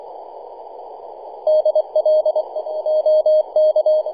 Morse code, heard as a keyed tone, about 25 wpm, 620 Hz. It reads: DL OK
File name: e51dwc-21-010mhz-cw